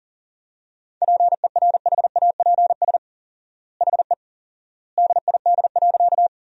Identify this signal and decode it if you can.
Morse code, heard as a keyed tone, about 40 wpm, 700 Hz. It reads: PERHAPS HE DID.
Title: 4x Word-Spacing - Sentences from Top 500 Words - 40wpm